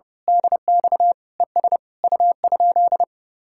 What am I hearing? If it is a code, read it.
Morse code, heard as a keyed tone, about 28 wpm, 700 Hz. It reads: DX ES U?
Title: Speed-Racing Common QSO Phrases Encore - 30wpm